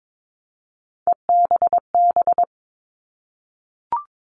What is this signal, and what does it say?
Morse code, heard as a keyed tone, about 22 wpm, 700 Hz. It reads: EBB
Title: All Cut Numbers - 3-Digit 22wpm